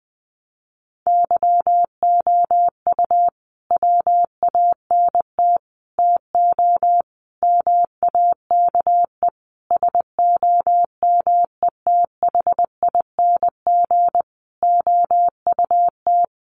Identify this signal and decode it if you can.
Morse code, heard as a keyed tone, about 20 wpm, 700 Hz. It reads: YOU WANT TO MAKE SOMETHING OUT
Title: Sentences from Top 600 Words 20wpm